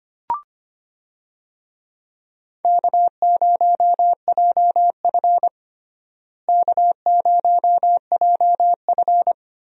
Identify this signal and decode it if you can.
Morse code, heard as a keyed tone, about 25 wpm, 700 Hz. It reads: K0JF K0JF
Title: USA Call Signs - 4 Characters - Repeated 2x - 25wpm